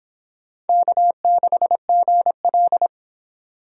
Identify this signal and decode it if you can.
Morse code, heard as a keyed tone, about 23 wpm, 700 Hz. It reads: K6GL